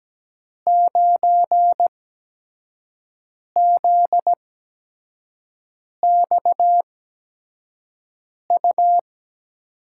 Morse code, heard as a keyed tone, about 17 wpm, spaced as for 5 wpm, 700 Hz. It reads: 9ZXU
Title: USA Call Signs - 5 Characters - 8x Character Spacing - 17wpm